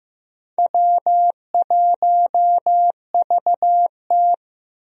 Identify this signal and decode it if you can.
Morse code, heard as a keyed tone, about 15 wpm, 700 Hz. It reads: W1VT
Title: USA Call Signs - 4 Characters - Repeated 3x - 15wpm